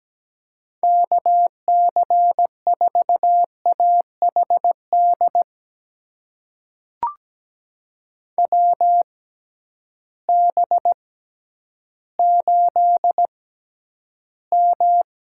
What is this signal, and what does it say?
Morse code, heard as a keyed tone, about 17 wpm, spaced as for 10 wpm, 700 Hz. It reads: KC4AHD WB8M
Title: USA Call Signs - 6 Characters - 6x Character Spacing - 17wpm